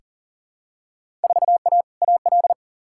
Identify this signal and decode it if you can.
Morse code, heard as a keyed tone, about 37 wpm, 700 Hz. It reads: 4A AL
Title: Field Day Exchanges - 40wpm